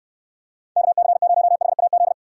Morse code, heard as a keyed tone, about 69 wpm, 700 Hz. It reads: KC0HAZ